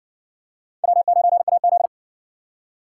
Morse code, heard as a keyed tone, about 58 wpm, 700 Hz. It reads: W9AZ